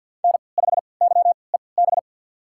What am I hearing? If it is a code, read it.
Morse code, heard as a keyed tone, about 50 wpm, spaced as for 28 wpm, 700 Hz. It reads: N5YEB
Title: USA Call Signs - 3x Character Spacing - 50wpm